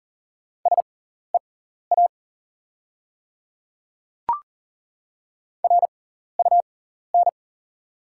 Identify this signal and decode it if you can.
Morse code, heard as a keyed tone, about 38 wpm, spaced as for 14 wpm, 690 Hz. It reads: SEA RUN
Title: 6x Character Spacing - Top 300 Words - 40wpm